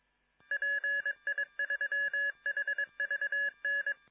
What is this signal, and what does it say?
Morse code, heard as a keyed tone, about 22 wpm, 1700 Hz. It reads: PI3HVN